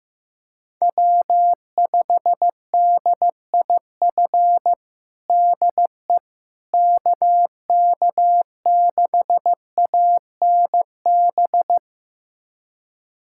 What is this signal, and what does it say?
Morse code, heard as a keyed tone, about 15 wpm, 700 Hz. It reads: W5DIF DE KK6ANB
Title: USA Callsign DE Callsign 15wpm